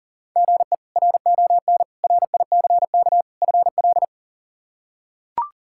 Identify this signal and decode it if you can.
Morse code, heard as a keyed tone, about 37 wpm, 700 Hz. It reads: GE RON RICK FL
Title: K1USN SST QSO Practice - 40wpm